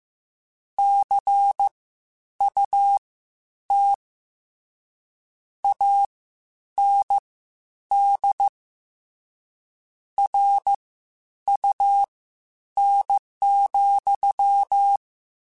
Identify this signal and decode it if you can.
Morse code, heard as a keyed tone, about 15 wpm, spaced as for 9 wpm, 800 Hz. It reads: CUT AND RUN,
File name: Misc / Misc-mp3s / WordPairs sp @13